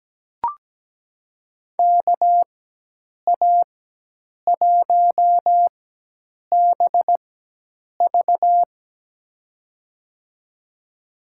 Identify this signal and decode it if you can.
Morse code, heard as a keyed tone, about 17 wpm, spaced as for 8 wpm, 700 Hz. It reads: KA1BV
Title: USA Call Signs - 5 Characters - 4x Character Spacing - 17wpm